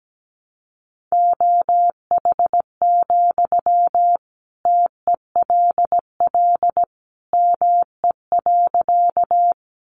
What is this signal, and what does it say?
Morse code, heard as a keyed tone, about 17 wpm, 700 Hz. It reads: OH, TELL ME.
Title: Sentences from Top 500 Words 17wpm